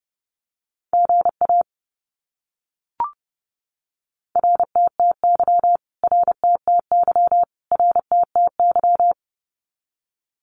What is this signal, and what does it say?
Morse code, heard as a keyed tone, about 30 wpm, 700 Hz. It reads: GA RTTY RTTY RTTY
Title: Abbreviations Repeated 3x - 30wpm